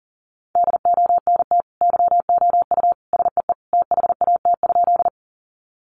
Morse code, heard as a keyed tone, about 40 wpm, 700 Hz. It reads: DONT YOU SEE THAT?
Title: Sentences from Top 300 Words 40wpm